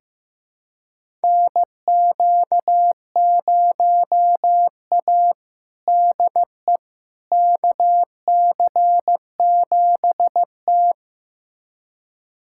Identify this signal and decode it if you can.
Morse code, heard as a keyed tone, about 15 wpm, 700 Hz. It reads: NQ0A DE KC7T
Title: USA Callsign DE Callsign - 4 Characters - 15wpm